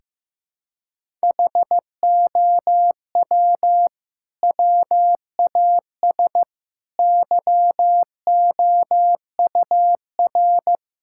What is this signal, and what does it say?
Morse code, heard as a keyed tone, about 15 wpm, 700 Hz. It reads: HOW WAS YOUR
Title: Sentences from Top 1600 Words 15wpm